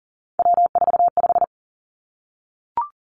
Morse code, heard as a keyed tone, about 42 wpm, 700 Hz. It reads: W45
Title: Three Challenging Letter-Number Combinations 40wpm